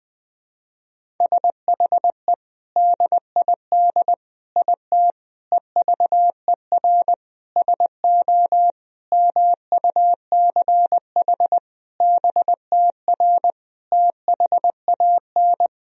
Morse code, heard as a keyed tone, about 20 wpm, 700 Hz. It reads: SHE DID IT EVER SO MUCH BTR THAN
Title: Sentences from Top 400 Words and Abbreviations 20wpm